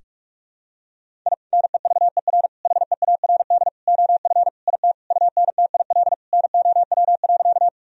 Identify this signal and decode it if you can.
Morse code, heard as a keyed tone, about 44 wpm, 700 Hz. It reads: I NEVER HEARD OF IT UNTIL NOW.
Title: Sentences from Top 1000 Words - Review Part 1 - 45wpm